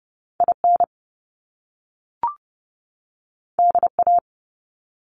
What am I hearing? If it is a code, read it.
Morse code, heard as a keyed tone, about 30 wpm, 700 Hz. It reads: IN DA
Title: Top Two Letter Combinations in Words 30wpm